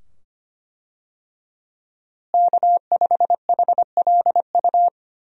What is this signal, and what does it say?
Morse code, heard as a keyed tone, about 25 wpm, 700 Hz. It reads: K5HLU
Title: USA Call Signs - 5 Characters - Repeated 2x - 25wpm